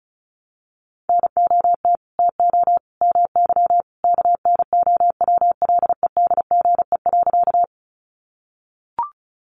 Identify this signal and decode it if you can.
Morse code, heard as a keyed tone, about 34 wpm, 700 Hz. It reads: NOT TO MY KNOWLEDGE.
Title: Sentences from Top 600 Words 35wpm